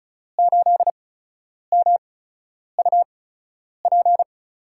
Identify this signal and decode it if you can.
Morse code, heard as a keyed tone, about 36 wpm, spaced as for 10 wpm, 700 Hz. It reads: 8MUP
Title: USA Call Signs - 5 Characters - 8x Character Spacing - 35wpm